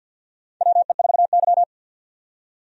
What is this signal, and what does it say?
Morse code, heard as a keyed tone, about 47 wpm, 700 Hz. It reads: WE4Y
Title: USA Call Signs - 4 Characters - 50wpm